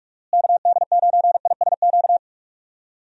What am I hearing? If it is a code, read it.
Morse code, heard as a keyed tone, about 41 wpm, 700 Hz. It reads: KD9ISQ
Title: USA Call Signs - 6 Characters - Repeated 3x - 45wpm